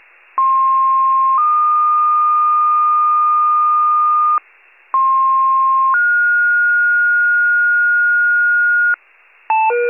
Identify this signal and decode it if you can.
Two-tone page, heard as a one-second tone, then a three-second tone, about 1060 then 1220 Hz, about 1060 then 1430 Hz, with noise. It is heard from 0.4 to 9.0 s.